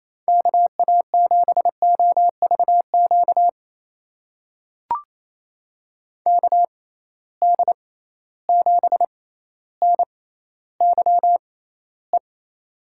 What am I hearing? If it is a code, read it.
Morse code, heard as a keyed tone, about 28 wpm, spaced as for 14 wpm, 700 Hz. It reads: KA7OVQ KD7NYE
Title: USA Call Signs - 6 Characters - 6x Character Spacing - 28wpm